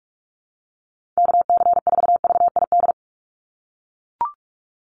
Morse code, heard as a keyed tone, about 44 wpm, 700 Hz. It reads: KC4VID
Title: USA Call Signs 45wpm